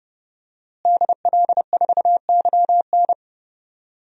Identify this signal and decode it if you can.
Morse code, heard as a keyed tone, about 32 wpm, 700 Hz. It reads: DL4YN
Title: DXCC-100 Call Signs - 5 Characters - Repeated 2x - 30wpm